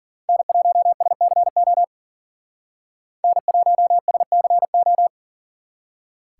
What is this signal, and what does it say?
Morse code, heard as a keyed tone, about 42 wpm, 700 Hz. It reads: N1SCO N1SCO
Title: Speed Racing - USA 1x3 Call Signs 40wpm